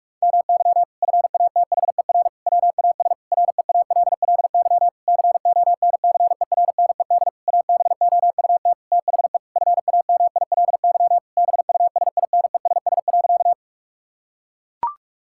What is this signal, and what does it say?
Morse code, heard as a keyed tone, about 42 wpm, 700 Hz. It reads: MY FATHER WAS REALLY CONCERNED ABOUT THE FAMILY BUSINESS.
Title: Sentences from Top 1000 Words - Review Part 6 - 45wpm